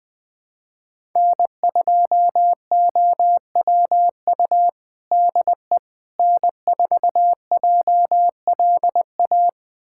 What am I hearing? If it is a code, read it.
Morse code, heard as a keyed tone, about 20 wpm, 700 Hz. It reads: N2OWU DE N4JLA
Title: USA Callsign DE Callsign - 5 Characters - 20wpm